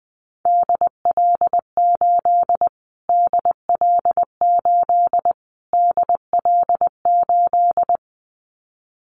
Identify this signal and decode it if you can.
Morse code, heard as a keyed tone, about 20 wpm, 700 Hz. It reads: DL8 DL8 DL8
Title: Callsign Prefixes for Top 25 DXCC - Repeated 3x - 20wpm